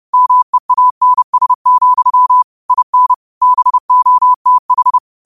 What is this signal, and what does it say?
Morse code, heard as a keyed tone, about 30 wpm, 1000 Hz. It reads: MEANS, IN BOTH
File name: qotdfiles / 30wpm